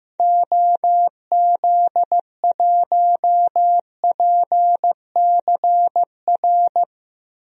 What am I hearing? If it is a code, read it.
Morse code, heard as a keyed tone, about 15 wpm, 700 Hz. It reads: OZ1PCR
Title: DXCC-100 Call Signs - 6 Characters - Repeated 2x - 15wpm